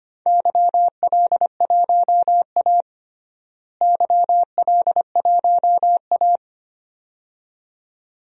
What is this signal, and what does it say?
Morse code, heard as a keyed tone, about 25 wpm, 700 Hz. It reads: YL1A YL1A
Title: DXCC-100 Call Signs - 4 Characters - Repeated 2x - 25wpm